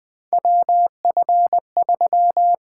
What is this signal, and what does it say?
Morse code, heard as a keyed tone, about 20 wpm, 700 Hz. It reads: WF3
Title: USA Call Signs - 4 Characters - 8x Character Spacing - 20wpm